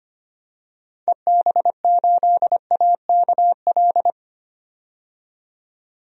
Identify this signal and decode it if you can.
Morse code, heard as a keyed tone, about 25 wpm, 700 Hz. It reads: EB8AKL
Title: DXCC-100 Call Signs - 6 Characters - 25wpm